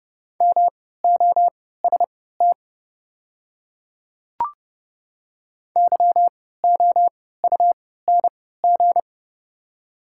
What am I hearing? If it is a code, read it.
Morse code, heard as a keyed tone, about 30 wpm, spaced as for 17 wpm, 700 Hz. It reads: MOST YOUNG
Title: 3x Character Spacing - Top 500 Words - 30wpm